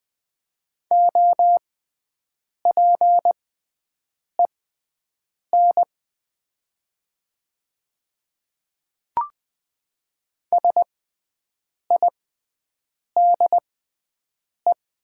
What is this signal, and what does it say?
Morse code, heard as a keyed tone, about 20 wpm, spaced as for 7 wpm, 700 Hz. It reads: OPEN SIDE
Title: 6x Character Spacing - Top 300 Words - 20wpm